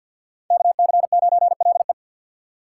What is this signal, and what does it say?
Morse code, heard as a keyed tone, about 47 wpm, 700 Hz. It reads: KC9LE
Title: USA Call Signs - 5 Characters - 4x Character Spacing - 50wpm